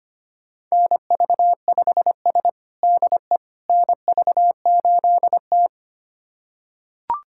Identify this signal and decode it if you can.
Morse code, heard as a keyed tone, about 25 wpm, 700 Hz. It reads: NV5S DE NV8T
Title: USA Callsign DE Callsign - 4 Characters - 25wpm